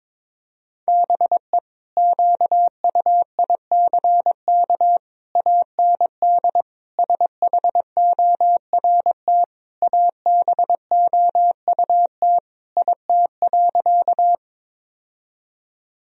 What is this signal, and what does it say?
Morse code, heard as a keyed tone, about 21 wpm, 700 Hz. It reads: BE QUICK AND SHORT ABOUT IT.
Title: Sentences from Top 1000 Words - Review Part 2 - 22wpm